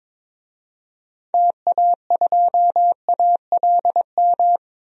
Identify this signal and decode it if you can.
Morse code, heard as a keyed tone, about 22 wpm, 700 Hz. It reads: TA2ALM
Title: DXCC-100 Call Signs - 6 Characters - 22wpm